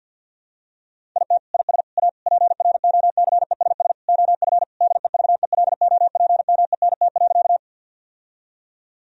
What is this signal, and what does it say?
Morse code, heard as a keyed tone, about 50 wpm, 700 Hz. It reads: IT IS A PROCESS OF DEVELOPMENT.
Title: Sentences from Top 1000 Words - Review Part 4 - 50wpm